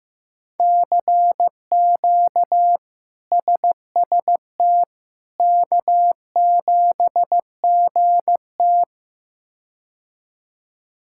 Morse code, heard as a keyed tone, about 16 wpm, 700 Hz. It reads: CQ SST K7GT
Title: K1USN SST QSO Practice - 15wpm